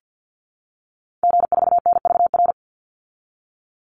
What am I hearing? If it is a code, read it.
Morse code, heard as a keyed tone, about 48 wpm, 700 Hz. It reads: G4NVR
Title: Call Signs Repeated 3x - 50wpm